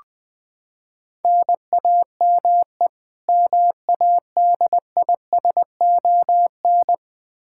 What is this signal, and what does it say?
Morse code, heard as a keyed tone, about 20 wpm, 700 Hz. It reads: NAME MADISON
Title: Practice QSO - Exchange 1 - Beginning - 20wpm